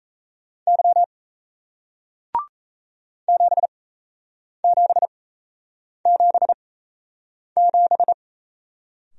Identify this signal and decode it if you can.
Morse code, heard as a keyed tone, about 36 wpm, spaced as for 8 wpm, 700 Hz. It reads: Y 7777